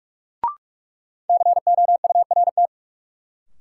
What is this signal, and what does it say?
Morse code, heard as a keyed tone, about 47 wpm, 700 Hz. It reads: COURT
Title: Sentences from Top 1600 Words - New Word Drill - 45wpm